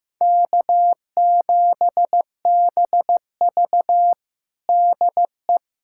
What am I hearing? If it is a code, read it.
Morse code, heard as a keyed tone, about 15 wpm, 700 Hz. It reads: K7BV DE
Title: Practice QSO - Exchange 4 - Ending - 15wpm